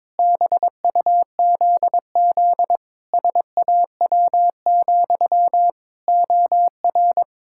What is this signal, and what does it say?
Morse code, heard as a keyed tone, about 22 wpm, 700 Hz. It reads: BUZZ SAW, OR
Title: Pangram Challenge v2 - 22wpm